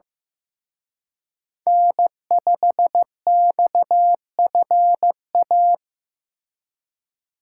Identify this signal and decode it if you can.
Morse code, heard as a keyed tone, about 18 wpm, 700 Hz. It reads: N5XFA
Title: Speed Racing - USA Call Signs 15wpm